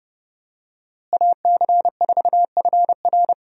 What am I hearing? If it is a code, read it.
Morse code, heard as a keyed tone, about 30 wpm, 700 Hz. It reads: AC4FR
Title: USA Call Signs - 5 Characters - Repeated 2x - 30wpm